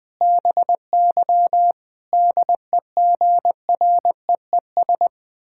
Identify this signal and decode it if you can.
Morse code, heard as a keyed tone, about 21 wpm, 700 Hz. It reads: BY DEGREES
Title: Pangram Challenge v2 - 20wpm